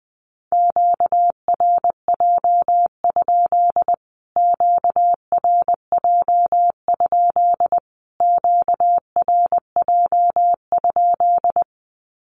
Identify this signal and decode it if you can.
Morse code, heard as a keyed tone, about 20 wpm, 700 Hz. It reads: QRJ? QRJ? QRJ?
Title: Q-Codes Repeated 3x 20wpm